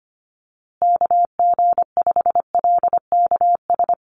Morse code, heard as a keyed tone, about 25 wpm, 700 Hz. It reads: KG5LKS
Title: USA Call Signs Repeated 3x 25wpm